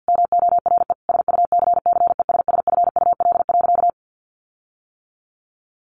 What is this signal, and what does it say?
Morse code, heard as a keyed tone, about 48 wpm, 700 Hz. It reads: MORE SUCCESSFUL.
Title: Sentences from Top 1500 Words 50wpm